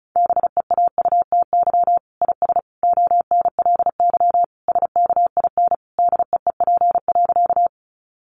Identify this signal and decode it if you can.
Morse code, heard as a keyed tone, about 34 wpm, 700 Hz. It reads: BEAUTY IS ONLY SKIN DEEP.
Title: Sentences from Top 1300 Words 35wpm